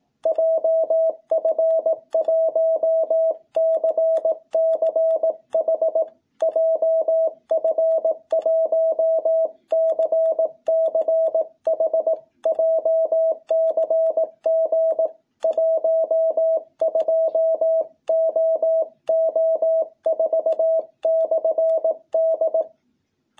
Morse code, heard as a keyed tone, about 18 wpm, 660 Hz. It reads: JF1CCH JF1CCH JCG 12OO4/D
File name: jh8cbh / 2021cw